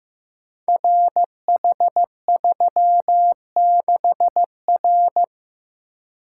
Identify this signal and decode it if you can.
Morse code, heard as a keyed tone, about 15 wpm, 700 Hz. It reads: RH36R
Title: Code Groups - 5 Letter:Numbers - 15wpm